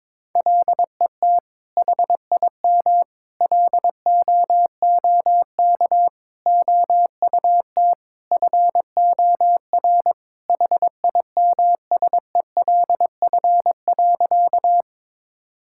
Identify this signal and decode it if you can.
Morse code, heard as a keyed tone, about 22 wpm, 700 Hz. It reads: LET HIM LOOK OUT FOR HIMSELF.
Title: Sentences from Top 1600 Words 22wpm